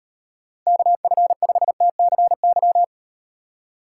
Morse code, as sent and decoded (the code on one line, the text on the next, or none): -.- ..-. ..... - -.-. -.--
KF5TCY